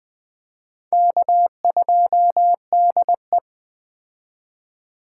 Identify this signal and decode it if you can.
Morse code, heard as a keyed tone, about 21 wpm, 700 Hz. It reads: K2DE
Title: Speed Racing - USA 1x2 Call Signs 17wpm